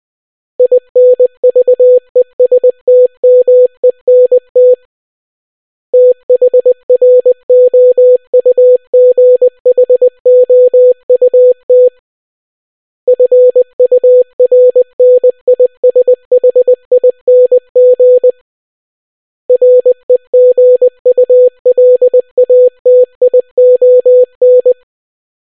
Morse code, heard as a keyed tone, about 20 wpm, 500 Hz. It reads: INVESTMENT THROUGHOUT FURNISHING REGULATION